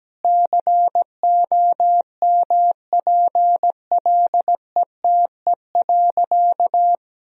Morse code, heard as a keyed tone, about 17 wpm, 700 Hz. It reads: COMPLETE.
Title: Sentences from Top 400 Words and Abbreviations 17wpm